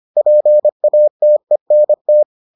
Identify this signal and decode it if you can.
Morse code, heard as a keyed tone, about 25 wpm, 600 Hz. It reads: PATENT